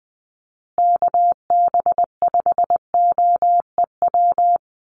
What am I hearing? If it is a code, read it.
Morse code, heard as a keyed tone, about 20 wpm, 700 Hz. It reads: KB5OEW
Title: USA Call Signs Repeated 3x 20wpm